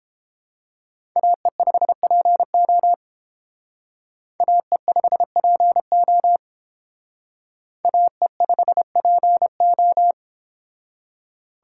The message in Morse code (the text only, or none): AE5PO AE5PO AE5PO